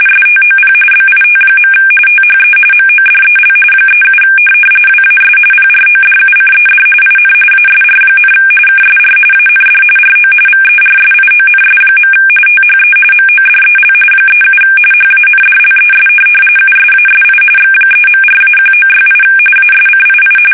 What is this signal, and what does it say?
RTTY, about 45.45 baud. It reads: STOP BIT. THE MODEL 37 WAS PRODUCED IN ASR, KSR AND RO VERSIONS. THE MODEL 37 HANDLES USASI X3.4-1968 ASCII
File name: fsk850 / teletype-corp